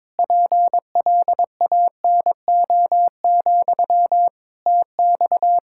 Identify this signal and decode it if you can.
Morse code, heard as a keyed tone, about 22 wpm, 700 Hz. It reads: PLANO, TX
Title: Practice QSO - Exchange 1 - Beginning - 22wpm